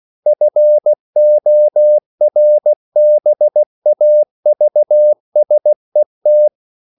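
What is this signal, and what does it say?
Morse code, heard as a keyed tone, about 16 wpm, 600 Hz. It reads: FORBAVSET
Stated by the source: Slow (QRS) Morse code at 16 wpm, from CW med Gnister.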